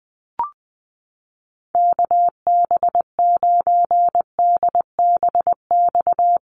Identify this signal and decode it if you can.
Morse code, heard as a keyed tone, about 20 wpm, 700 Hz. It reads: KB9DBX
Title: USA Call Signs Repeated 3x 20wpm